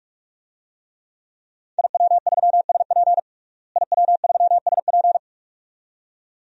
Morse code, heard as a keyed tone, about 42 wpm, 700 Hz. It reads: IW3SP IW3SP
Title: DXCC-100 Call Signs - 5 Characters - Repeated 2x - 45wpm